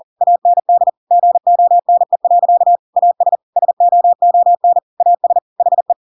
Morse code, heard as a keyed tone, about 40 wpm, 700 Hz. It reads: AND GONE. AS SOON AS HE
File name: Pinocchio / Pinocchio ch7 40 40 ik1hge